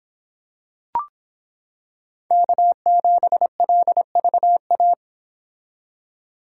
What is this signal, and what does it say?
Morse code, heard as a keyed tone, about 25 wpm, 700 Hz. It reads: K7LVA